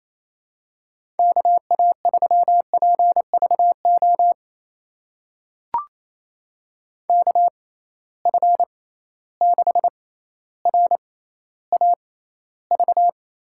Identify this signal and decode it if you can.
Morse code, heard as a keyed tone, about 28 wpm, spaced as for 14 wpm, 700 Hz. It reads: KA3PVO KF6RAV
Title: USA Call Signs - 6 Characters - 6x Character Spacing - 28wpm